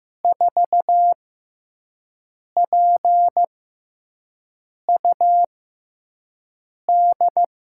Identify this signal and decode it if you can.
Morse code, heard as a keyed tone, about 15 wpm, spaced as for 5 wpm, 700 Hz. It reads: 4PUD
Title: USA Call Signs - 5 Characters - 6x Character Spacing - 15wpm